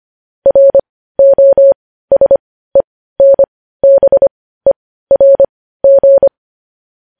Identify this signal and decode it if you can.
Morse code, heard as a keyed tone, about 25 wpm, spaced as for 15 wpm, 550 Hz. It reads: ROSENBERG
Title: News Headlines at 15